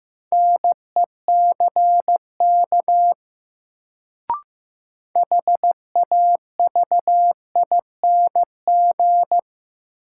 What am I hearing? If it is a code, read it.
Morse code, heard as a keyed tone, about 15 wpm, 700 Hz. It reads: NECK HAVING
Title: Sets of 5 Words Encore - New Word Drill - 15wpm